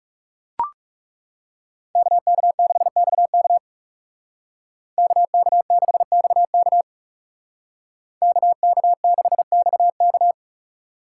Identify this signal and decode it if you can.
Morse code, heard as a keyed tone, about 39 wpm, 700 Hz. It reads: KK6XK KK6XK KK6XK